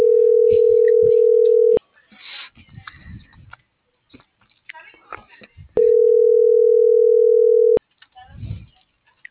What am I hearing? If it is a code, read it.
Ringback tone (North America), 440 + 480 Hz.